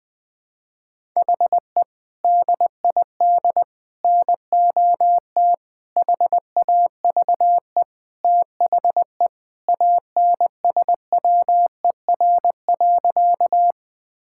Morse code, heard as a keyed tone, about 21 wpm, 700 Hz. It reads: HE DID NOT HAVE THE ANSWER.
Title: Sentences with 4-Word Phrases - 20wpm